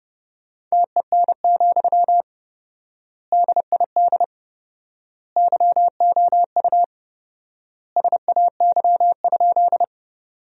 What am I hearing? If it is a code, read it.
Morse code, heard as a keyed tone, about 30 wpm, 700 Hz. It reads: TEN, DID YOU SAY?
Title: 4x Word-Spacing - Sentences from Top 400 Words - 30wpm